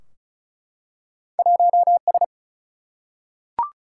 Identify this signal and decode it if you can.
Morse code, heard as a keyed tone, about 36 wpm, 700 Hz. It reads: 1S
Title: Single-Letter Single-Number 35wpm